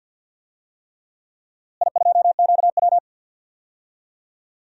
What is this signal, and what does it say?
Morse code, heard as a keyed tone, about 52 wpm, 700 Hz. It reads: I2QW
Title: DXCC-100 Call Signs - 4 Characters - Repeated 3x - 50wpm